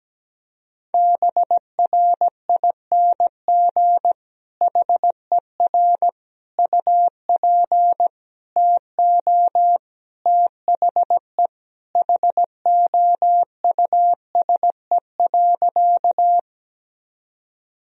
Morse code, heard as a keyed tone, about 17 wpm, 700 Hz. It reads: BRING HER UP TO THE HOUSE.
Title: Sentences from Top 1000 Words - Review Part 3 - 17wpm